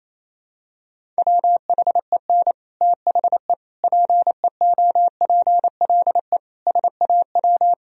Morse code, heard as a keyed tone, about 28 wpm, 700 Hz. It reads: WHEN THE PEOPLE SAW
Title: Sentences from Top 1000 Words - Review Part 1 - 28wpm